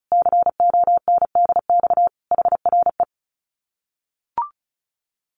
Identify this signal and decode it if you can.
Morse code, heard as a keyed tone, about 34 wpm, 700 Hz. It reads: CONDX HRE
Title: Common QSO Phrases 35wpm